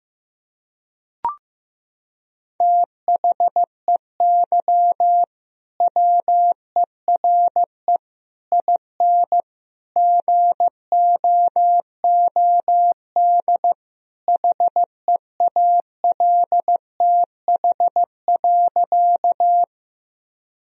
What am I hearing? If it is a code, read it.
Morse code, heard as a keyed tone, about 15 wpm, 700 Hz. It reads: THEY WERE IN GOOD HEALTH.
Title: Sentences from Top 1000 Words - Review Part 3 - 15wpm